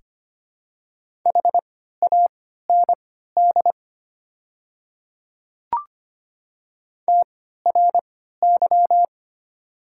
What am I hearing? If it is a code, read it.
Morse code, heard as a keyed tone, about 25 wpm, spaced as for 14 wpm, 700 Hz. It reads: HAND TRY
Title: 3x Character Spacing - Top 300 Words - 25wpm